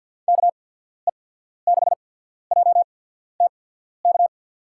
Morse code, heard as a keyed tone, about 53 wpm, spaced as for 14 wpm, 700 Hz. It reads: KE6JTK
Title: USA Call Signs - 6 Characters - 8x Character Spacing - 50wpm